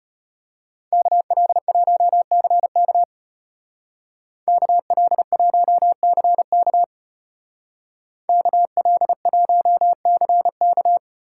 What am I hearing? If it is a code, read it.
Morse code, heard as a keyed tone, about 31 wpm, 700 Hz. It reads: KL1CK KL1CK KL1CK
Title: Speed Racing - USA 2x2 Call Signs 25wpm